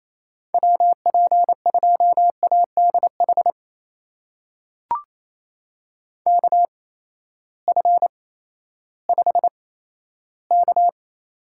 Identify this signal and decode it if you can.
Morse code, heard as a keyed tone, about 28 wpm, spaced as for 14 wpm, 700 Hz. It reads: WP2ADH KF5K
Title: USA Call Signs - 6 Characters - 8x Character Spacing - 28wpm